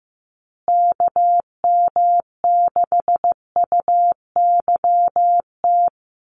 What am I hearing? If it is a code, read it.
Morse code, heard as a keyed tone, about 15 wpm, 700 Hz. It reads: KM6UYT
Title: USA Call Signs Repeated 2x 15wpm